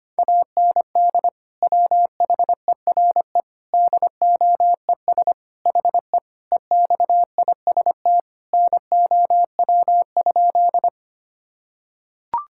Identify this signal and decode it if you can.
Morse code, heard as a keyed tone, about 25 wpm, 700 Hz. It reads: AND WHERE DOES HE EXIST NOW?
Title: Sentences from Top 1000 Words - Review Part 5 - 25wpm